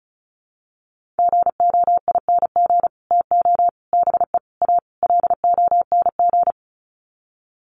Morse code, heard as a keyed tone, about 34 wpm, 700 Hz. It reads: GOING TO BE A LONG
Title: Sets of 5 Words Encore2 35wpm